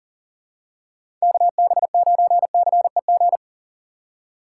Morse code, heard as a keyed tone, about 37 wpm, 700 Hz. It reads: KB9CEG
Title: USA Call Signs - 4x Character Spacing - 40wpm